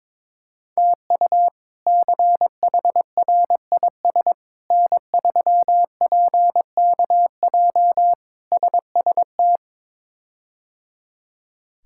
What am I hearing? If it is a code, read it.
Morse code, heard as a keyed tone, about 22 wpm, 700 Hz. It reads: TU CHRIS N3PKJ SST
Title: K1USN SST QSO Practice - 22wpm